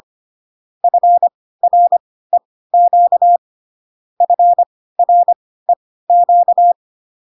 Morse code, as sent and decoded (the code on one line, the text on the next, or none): ..-. .-. --.- / ..-. .-. --.-
FREQ FREQ